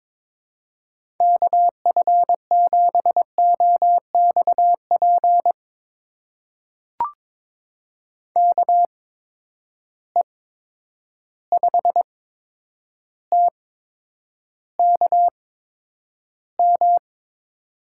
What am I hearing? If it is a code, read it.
Morse code, heard as a keyed tone, about 22 wpm, spaced as for 9 wpm, 700 Hz. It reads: KF7OXP KE5TKM
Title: USA Call Signs - 6 Characters - 8x Character Spacing - 22wpm